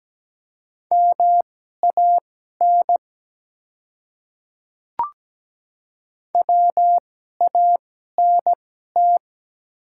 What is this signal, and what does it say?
Morse code, heard as a keyed tone, about 17 wpm, spaced as for 12 wpm, 700 Hz. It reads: MAN WANT
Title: 2x Character Spacing - Top 200 Words - 17wpm